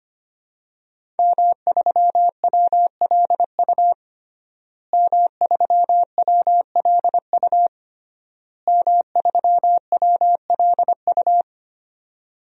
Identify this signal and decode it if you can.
Morse code, heard as a keyed tone, about 25 wpm, 700 Hz. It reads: M3WLU M3WLU M3WLU
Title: DXCC-100 Call Signs - 5 Characters - Repeated 3x - 25wpm